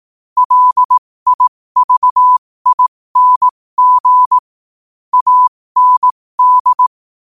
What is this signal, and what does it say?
Morse code, heard as a keyed tone, about 18 wpm, spaced as for 15 wpm, 1000 Hz. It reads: LIVING AND
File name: 15wpm